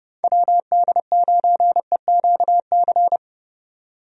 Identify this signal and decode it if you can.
Morse code, heard as a keyed tone, about 30 wpm, 700 Hz. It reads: WD9EQC
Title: USA Call Signs - 6 Characters - 30wpm